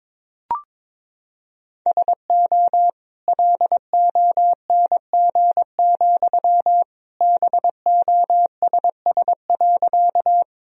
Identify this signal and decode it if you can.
Morse code, heard as a keyed tone, about 22 wpm, 700 Hz. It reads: SO LONG, BOSS.
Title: Sentences from Top 1000 Words - Review Part 4 - 22wpm